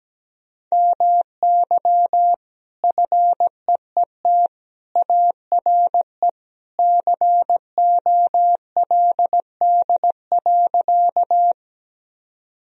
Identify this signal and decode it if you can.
Morse code, heard as a keyed tone, about 17 wpm, 700 Hz. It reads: MY FEET ARE COLD.
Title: Sentences from Top 1000 Words - Review Part 1 - 17wpm